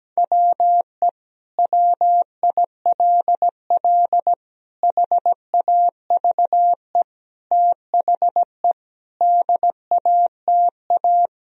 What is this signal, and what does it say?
Morse code, heard as a keyed tone, about 17 wpm, 700 Hz. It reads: WE WILL HAVE THE DATA
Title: Sentences from Top 1000 Words - Review Part 3 - 17wpm